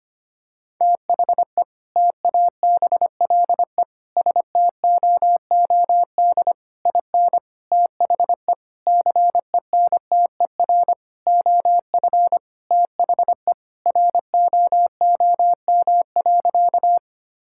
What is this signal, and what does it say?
Morse code, heard as a keyed tone, about 25 wpm, 700 Hz. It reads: THE TABLE STOOD IN THE CENTER OF THE ROOM.
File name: practice / 25wpm / Sentences with 4-Word Phrases - 25wpm